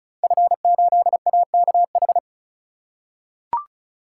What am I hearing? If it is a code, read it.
Morse code, heard as a keyed tone, about 36 wpm, 700 Hz. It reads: F8AKH
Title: Code Groups - 5 Letter:Numbers - 35wpm